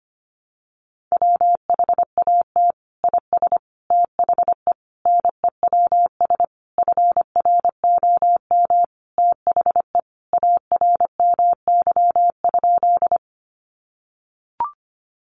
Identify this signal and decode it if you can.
Morse code, heard as a keyed tone, about 25 wpm, 700 Hz. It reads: WHAT IS THE NEWS FROM THE ARMY?
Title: Sentences from Top 800 Words 25wpm